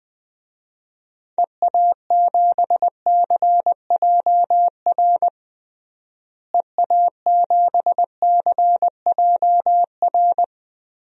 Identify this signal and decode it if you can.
Morse code, heard as a keyed tone, about 20 wpm, 700 Hz. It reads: EA7CJR EA7CJR
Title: DXCC-100 Call Signs - 6 Characters - Repeated 3x - 20wpm